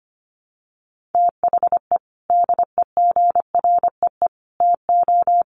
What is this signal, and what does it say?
Morse code, heard as a keyed tone, about 25 wpm, 700 Hz. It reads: THE DEGREE TO
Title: Sets of 5 Words Encore2 25wpm